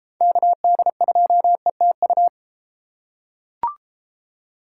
Morse code, heard as a keyed tone, about 31 wpm, 700 Hz. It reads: KD2ETU